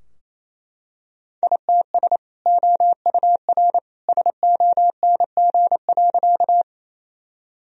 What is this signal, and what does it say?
Morse code, heard as a keyed tone, about 28 wpm, 700 Hz. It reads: ITS OUR SONG.
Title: Sentences from Top 1000 Words - Review Part 2 - 28wpm